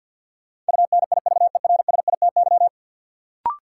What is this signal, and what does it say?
Morse code, heard as a keyed tone, about 52 wpm, 700 Hz. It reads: UNIVERSITY
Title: Sentences from Top 1300 Words - New Word Drill Alt - 50wpm